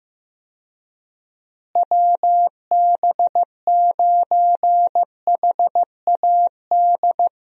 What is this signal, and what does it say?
Morse code, heard as a keyed tone, about 15 wpm, 700 Hz. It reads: WB9HAD
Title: USA Call Signs - 6 Characters - Repeated 3x - 15wpm